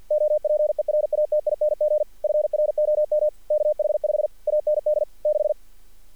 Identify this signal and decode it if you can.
Morse code, heard as a keyed tone, about 48 wpm, 610 Hz. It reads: OPERATING FROM KH5 AND 6
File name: archived / CRSnet